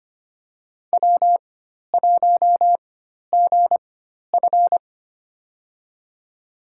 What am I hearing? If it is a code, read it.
Morse code, heard as a keyed tone, about 25 wpm, spaced as for 12 wpm, 700 Hz. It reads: W1GF
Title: USA Call Signs - 4x Character Spacing - 25wpm